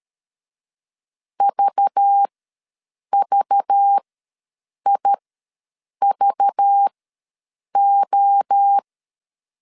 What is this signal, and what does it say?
Morse code, heard as a keyed tone, about 13 wpm, spaced as for 7 wpm, 800 Hz. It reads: VVIVO
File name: LEZIONI / 5 LBVC / 5 LBV ws 12wpm 3015